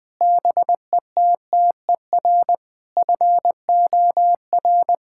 Morse code, heard as a keyed tone, about 20 wpm, 700 Hz. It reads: BETTER FOR
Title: Sentences from Top 1000 Words - Review Part 5 - 20wpm